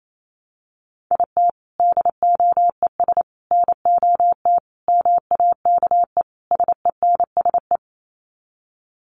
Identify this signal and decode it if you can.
Morse code, heard as a keyed tone, about 28 wpm, 700 Hz. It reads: IT DOES NOT MAKE SENSE
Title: Sets of 5 Words Encore2 28wpm